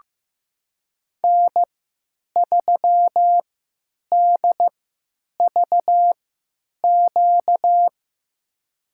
Morse code, heard as a keyed tone, about 15 wpm, spaced as for 8 wpm, 700 Hz. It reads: N3DVQ
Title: USA Call Signs - 3x Character Spacing - 15wpm